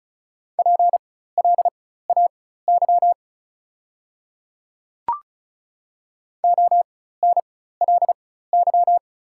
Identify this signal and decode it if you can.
Morse code, heard as a keyed tone, about 36 wpm, spaced as for 16 wpm, 700 Hz. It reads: PLAY ONLY
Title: 4x Character Spacing - Top 200 Words - 35wpm